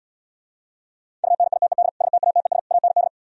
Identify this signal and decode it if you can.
Morse code, heard as a keyed tone, about 77 wpm, 700 Hz. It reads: UNITED STATES AND